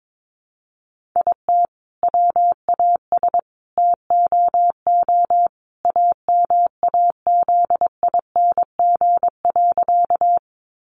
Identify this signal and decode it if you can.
Morse code, heard as a keyed tone, about 22 wpm, 700 Hz. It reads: IT WAS TOO AMAZING.
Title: Sentences from Top 1500 Words 22wpm